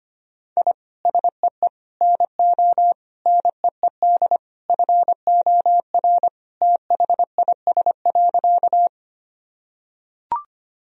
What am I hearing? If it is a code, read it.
Morse code, heard as a keyed tone, about 25 wpm, 700 Hz. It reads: I SEE NO NEED FOR THIS.
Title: Sentences from Top 1000 Words - Review Part 1 - 25wpm